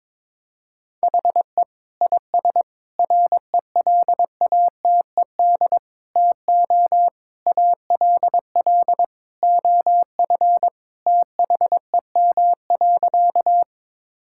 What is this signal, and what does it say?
Morse code, heard as a keyed tone, about 22 wpm, 700 Hz. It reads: HE IS RELATED TO ALL OF THEM.
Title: Sentences from Top 1000 Words - Review Part 5 - 22wpm